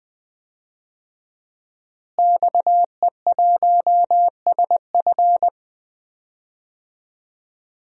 Morse code, heard as a keyed tone, about 21 wpm, 700 Hz. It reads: XE1SF
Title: DXCC-100 Call Signs - 5 Characters - Repeated 3x - 20wpm